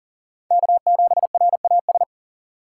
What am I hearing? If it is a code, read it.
Morse code, heard as a keyed tone, about 40 wpm, 700 Hz. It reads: K7RAS
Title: USA Call Signs - 5 Characters - 3x Character Spacing - 40wpm